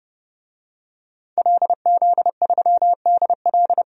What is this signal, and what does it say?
Morse code, heard as a keyed tone, about 31 wpm, 700 Hz. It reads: LZ3DL
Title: DXCC-100 Call Signs - 5 Characters - 30wpm